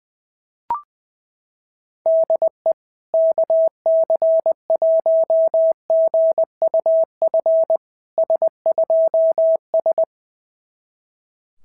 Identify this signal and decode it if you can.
Morse code, heard as a keyed tone, about 20 wpm, 660 Hz. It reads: DE KC1GUF S2S
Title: SOTA QSO Practice 20wpm